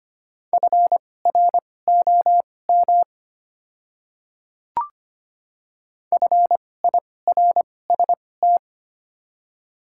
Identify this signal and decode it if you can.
Morse code, heard as a keyed tone, about 25 wpm, spaced as for 18 wpm, 700 Hz. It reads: FROM FIRST
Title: 2x Character Spacing - Top 100 Words - 25wpm